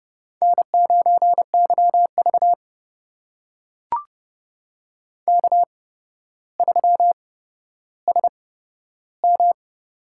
Morse code, heard as a keyed tone, about 30 wpm, spaced as for 13 wpm, 700 Hz. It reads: N9YV K3SM
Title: USA Call Signs - 4 Characters - 8x Character Spacing - 30wpm